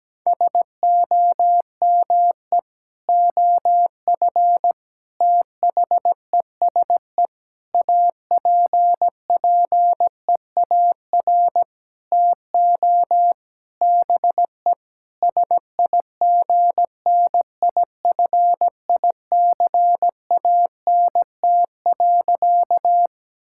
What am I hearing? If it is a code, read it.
Morse code, heard as a keyed tone, about 17 wpm, 700 Hz. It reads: SOME OF THESE APPEAR TO BE SIGNIFICANT.
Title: Sentences from Top 1000 Words - Review Part 6 - 17wpm